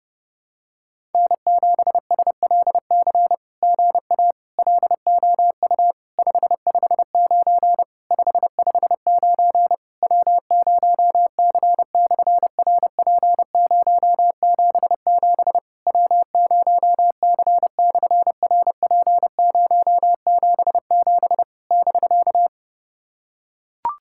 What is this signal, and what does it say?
Morse code, heard as a keyed tone, about 31 wpm, 700 Hz. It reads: N7SLC GA LOU 559 559 W0C/RP077 W0C/RP077 <BK>
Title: SOTA QSO Practice 30wpm